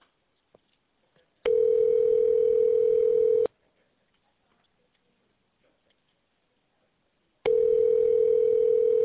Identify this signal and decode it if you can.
Ringback tone (North America), 440 + 480 Hz.